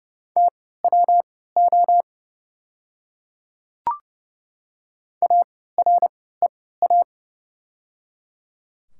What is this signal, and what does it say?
Morse code, heard as a keyed tone, about 30 wpm, spaced as for 17 wpm, 690 Hz. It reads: TWO AREA
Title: 3x Character Spacing - Top 500 Words - 30wpm